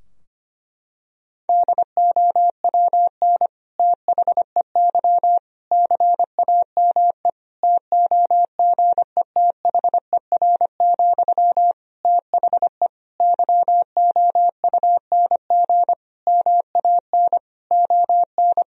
Morse code, heard as a keyed tone, about 25 wpm, 700 Hz. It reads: DOWN THEY CAME TOGETHER, THE YOUNG MAN ON
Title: Sentences from Top 1000 Words - Review Part 2 - 25wpm